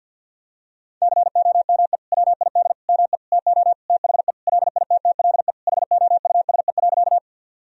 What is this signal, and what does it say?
Morse code, heard as a keyed tone, about 46 wpm, 700 Hz. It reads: COME WID ME TO THE LITTLE HOUSE.
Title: Sentences from Top 200 Words and Abbreviations 50wpm